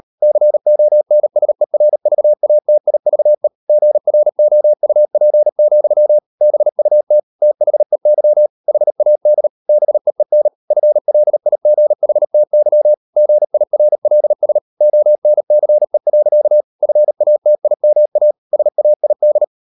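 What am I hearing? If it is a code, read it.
Morse code, heard as a keyed tone, about 38 wpm, 600 Hz. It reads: CONSERVATIVE GROUP, BUT THEY HAD BEEN FLIGHTY GIRLS ONCE. FATIMA SAID
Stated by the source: Morse code practice at 38 WPM, sending Ken Follett's novel Never in English, from CW med Gnister.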